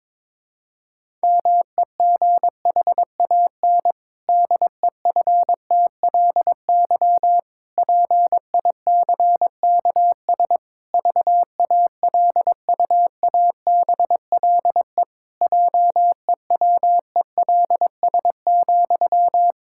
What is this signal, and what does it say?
Morse code, heard as a keyed tone, about 22 wpm, 700 Hz. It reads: MEGHAN DEFTLY PICKS VALUABLE JEWELS,
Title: Pangram Challenge v2 - 22wpm